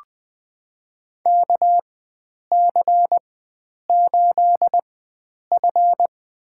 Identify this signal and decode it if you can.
Morse code, heard as a keyed tone, about 20 wpm, spaced as for 9 wpm, 700 Hz. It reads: KC8F
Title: USA Call Signs - 4x Character Spacing - 20wpm